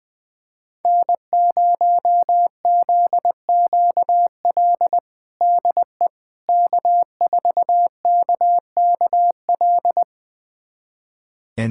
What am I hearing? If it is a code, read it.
Morse code, heard as a keyed tone, about 20 wpm, 700 Hz. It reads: N0ZQL DE K4KKL
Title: USA Callsign DE Callsign - 5 Characters - 20wpm